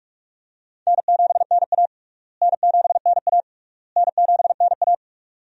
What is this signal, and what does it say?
Morse code, heard as a keyed tone, about 42 wpm, 700 Hz. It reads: N7NA N7NA N7NA
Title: USA Call Signs - 4 Characters - Repeated 3x - 45wpm